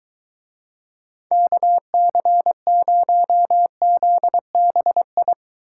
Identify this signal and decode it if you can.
Morse code, heard as a keyed tone, about 22 wpm, 700 Hz. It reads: KC0ZBI